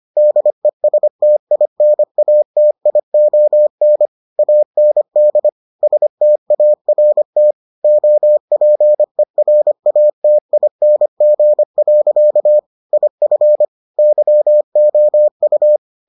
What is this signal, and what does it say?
Morse code, heard as a keyed tone, about 25 wpm, 600 Hz. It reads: DESTINATION AND START OPERATING. IF YOU